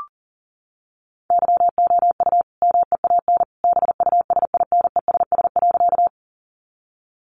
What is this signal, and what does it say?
Morse code, heard as a keyed tone, about 40 wpm, 700 Hz. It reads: YOU MEAN BUSINESS.
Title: Sentences from Top 600 Words 40wpm